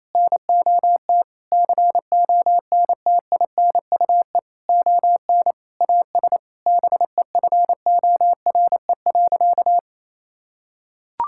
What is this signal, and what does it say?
Morse code, heard as a keyed tone, about 27 wpm, 700 Hz. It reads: NOT CONTINUE ON AS BEFORE.
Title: Sentences from Top 1000 Words - Review Part 6 - 28wpm